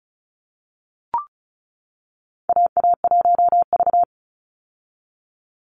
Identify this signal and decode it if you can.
Morse code, heard as a keyed tone, about 36 wpm, 700 Hz. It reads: AA1V
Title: Call Signs - 35wpm